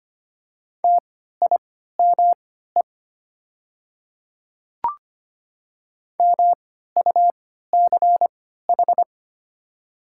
Morse code, heard as a keyed tone, about 25 wpm, spaced as for 14 wpm, 700 Hz. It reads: TIME MUCH
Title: 3x Character Spacing - Top 300 Words - 25wpm